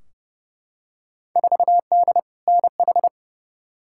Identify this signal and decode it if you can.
Morse code, heard as a keyed tone, about 30 wpm, 700 Hz. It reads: 4D NH
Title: Field Day Exchanges Encore - 30wpm